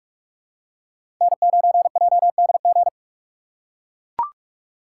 Morse code, heard as a keyed tone, about 44 wpm, 700 Hz. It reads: N9JDG
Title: USA Call Signs - 5 Characters - 4x Character Spacing - 45wpm